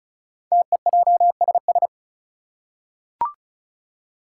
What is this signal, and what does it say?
Morse code, heard as a keyed tone, about 36 wpm, 700 Hz. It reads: TEJSS
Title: Code Groups - 5 Letters - 35wpm